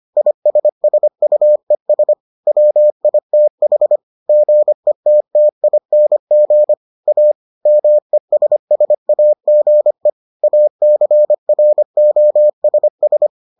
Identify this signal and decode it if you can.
Morse code, heard as a keyed tone, about 25 wpm, 600 Hz. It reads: ISSUES WITH GETTING A MESSAGE ACROSS